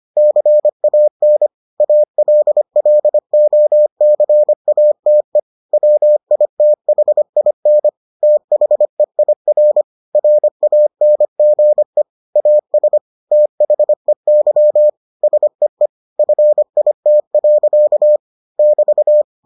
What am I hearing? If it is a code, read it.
Morse code, heard as a keyed tone, about 25 wpm, 600 Hz. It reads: CAN ALLOCATE WITHIN THEIR RANGE AS THEY SEE FIT. <BT>